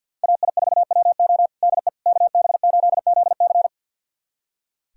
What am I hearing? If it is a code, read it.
Morse code, heard as a keyed tone, about 46 wpm, 700 Hz. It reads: AI4WO DE KB8ZC
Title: USA Callsign DE Callsign - 5 Characters - 50wpm